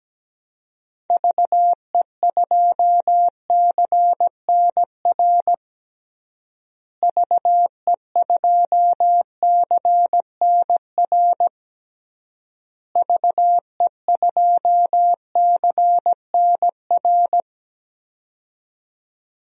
Morse code, heard as a keyed tone, about 17 wpm, 700 Hz. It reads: VE2CNR VE2CNR VE2CNR
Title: DXCC-100 Call Signs - 6 Characters - Repeated 3x - 17wpm